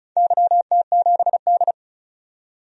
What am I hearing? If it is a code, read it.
Morse code, heard as a keyed tone, about 36 wpm, 700 Hz. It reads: YT7D
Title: DXCC-100 Call Signs - 4 Characters - 35wpm